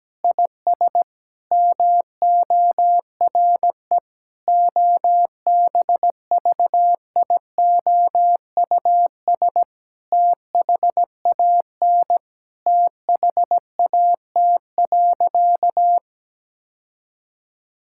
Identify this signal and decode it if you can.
Morse code, heard as a keyed tone, about 17 wpm, 700 Hz. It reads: IS MORE OBVIOUS THAN THAT.
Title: Sentences from Top 1600 Words 17wpm